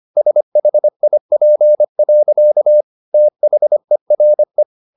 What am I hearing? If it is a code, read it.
Morse code, heard as a keyed tone, about 25 wpm, 600 Hz. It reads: SHIP. THERE